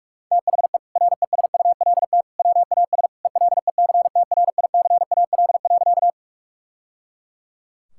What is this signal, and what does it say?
Morse code, heard as a keyed tone, about 46 wpm, 700 Hz. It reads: THE RESULT WAS ELECTRICAL.
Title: Sentences from Top 1000 Words - Review Part 6 - 45wpm